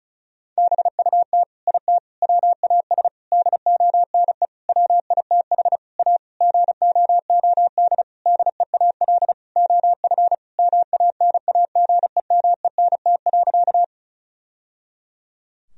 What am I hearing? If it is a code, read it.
Morse code, heard as a keyed tone, about 36 wpm, 700 Hz. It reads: BUT IT WAS DONE WITH A GOOD DEAL OF MANAGEMENT.
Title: Sentences from Top 1000 Words - Review Part 4 - 35wpm